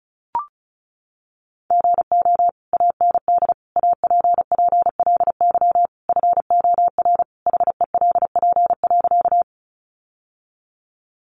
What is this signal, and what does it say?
Morse code, heard as a keyed tone, about 34 wpm, 700 Hz. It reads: GO AND APPLY FOR HELP.
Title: Sentences from Top 1200 Words 35wpm